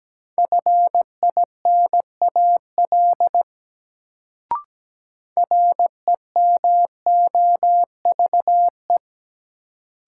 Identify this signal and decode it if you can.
Morse code, heard as a keyed tone, about 17 wpm, 700 Hz. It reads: FINAL REMOVE
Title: Sentences from Top 800 Words - New Word Drill - 17wpm